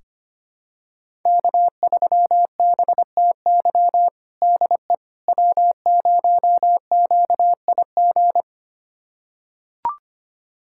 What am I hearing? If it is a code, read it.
Morse code, heard as a keyed tone, about 25 wpm, 700 Hz. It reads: K3BTY DE W0QIG
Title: USA Callsign DE Callsign - 5 Characters - 25wpm